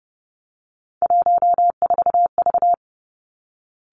Morse code, heard as a keyed tone, about 30 wpm, 700 Hz. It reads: 14V
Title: Three Challenging Letter-Number Combinations 30wpm